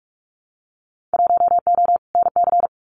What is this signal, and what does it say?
Morse code, heard as a keyed tone, about 44 wpm, 700 Hz. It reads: 1O NC
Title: Winter Field Day Exchanges 45wpm